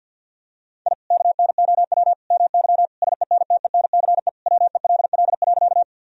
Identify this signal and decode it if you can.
Morse code, heard as a keyed tone, about 50 wpm, 700 Hz. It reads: I KNOW MY SENTENCE WELL.
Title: Sentences from Top 1000 Words - Review Part 1 - 50wpm